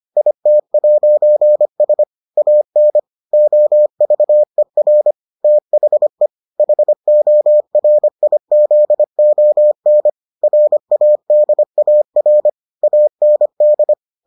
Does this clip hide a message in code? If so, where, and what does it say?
Yes: Morse code from 0.2 to 13.9 s, heard as a keyed tone, about 25 wpm, 600 Hz. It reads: IT'S AN OVER THE HORIZON RADAR AND